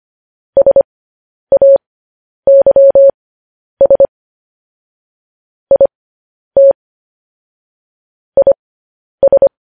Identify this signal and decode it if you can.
Morse code, heard as a keyed tone, about 25 wpm, spaced as for 10 wpm, 550 Hz. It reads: SAYS IT IS
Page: news-headlines-in-morse-code-at-10-wpm